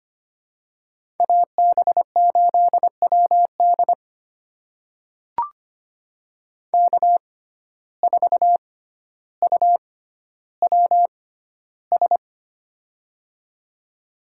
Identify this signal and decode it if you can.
Morse code, heard as a keyed tone, about 25 wpm, spaced as for 13 wpm, 700 Hz. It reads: AB8WD K4UWS
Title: USA Call Signs - 5 Characters - 6x Character Spacing - 25wpm